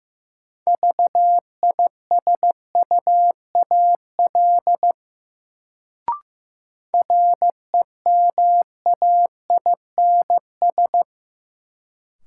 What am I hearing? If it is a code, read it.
Morse code, heard as a keyed tone, about 15 wpm, 700 Hz. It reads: VISUAL REMAINS
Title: Sets of 5 Words Encore - New Word Drill - 15wpm